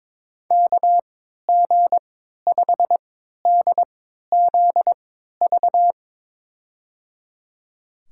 Morse code, heard as a keyed tone, about 22 wpm, spaced as for 13 wpm, 700 Hz. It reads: KG5DZV
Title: USA Call Signs - 3x Character Spacing - 22wpm